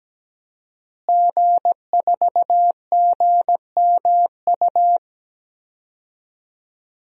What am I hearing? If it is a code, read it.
Morse code, heard as a keyed tone, about 17 wpm, 700 Hz. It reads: G4GMU
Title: DXCC-100 Call Signs - 5 Characters - Repeated 3x - 17wpm